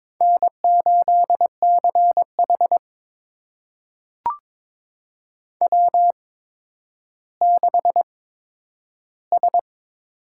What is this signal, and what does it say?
Morse code, heard as a keyed tone, about 22 wpm, spaced as for 11 wpm, 700 Hz. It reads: N8CH W6S
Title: USA Call Signs - 4 Characters - 8x Character Spacing - 22wpm